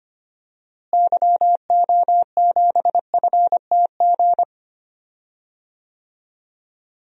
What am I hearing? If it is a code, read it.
Morse code, heard as a keyed tone, about 25 wpm, 700 Hz. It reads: YO7FTG
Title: DXCC-100 Call Signs - 6 Characters - Repeated 3x - 25wpm